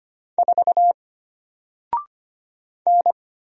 Morse code, heard as a keyed tone, about 25 wpm, 700 Hz. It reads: 4 N
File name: Single-Letter-Number-25wpm